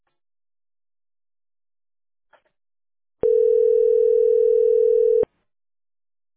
Ringback tone (North America), 440 + 480 Hz.